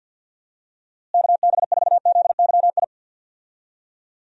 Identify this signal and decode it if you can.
Morse code, heard as a keyed tone, about 52 wpm, 700 Hz. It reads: KB4ZYI
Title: USA Call Signs - 6 Characters - Repeated 2x - 50wpm